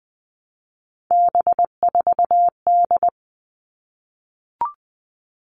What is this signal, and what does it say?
Morse code, heard as a keyed tone, about 20 wpm, 700 Hz. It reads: B4D
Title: All Cut Numbers - 3-Digit 20wpm